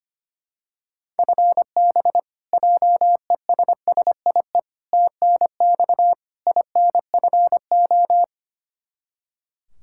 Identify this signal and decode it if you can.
Morse code, heard as a keyed tone, about 25 wpm, 700 Hz. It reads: FB JESSIE TNX INFO
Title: Practice QSO - Exchange 3 - More Info - 25wpm